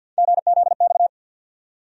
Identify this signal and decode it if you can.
Morse code, heard as a keyed tone, about 48 wpm, 700 Hz. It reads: GZX
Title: USA Call Sign Postfixes - 3 Letters 50wpm